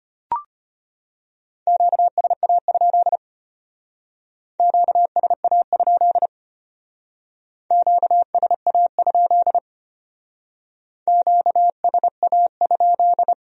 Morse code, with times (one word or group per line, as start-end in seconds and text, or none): 1.7-3.2 QSA?
4.6-6.3 QSA?
7.7-9.6 QSA?
11.1-13.3 QSA?